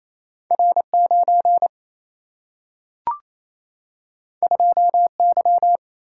Morse code, heard as a keyed tone, about 28 wpm, 700 Hz. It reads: R 9 2 Y
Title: Single-Letter Single-Number 28wpm